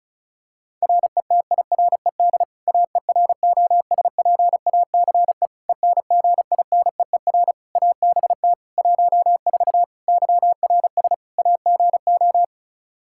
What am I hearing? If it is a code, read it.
Morse code, heard as a keyed tone, about 34 wpm, 700 Hz. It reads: RETIRED AEROSPACE ENGINEER ABT 14 YRS AGO
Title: K1USN SST QSO Practice 35wpm